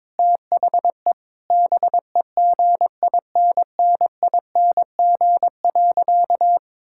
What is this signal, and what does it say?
Morse code, heard as a keyed tone, about 22 wpm, 700 Hz. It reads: THE BEGINNING.
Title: Sentences from Top 1000 Words - Review Part 5 - 22wpm